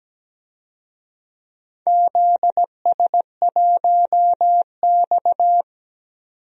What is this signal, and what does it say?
Morse code, heard as a keyed tone, about 17 wpm, 700 Hz. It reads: ZS1X
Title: DXCC-100 Call Signs - 4 Characters - 17wpm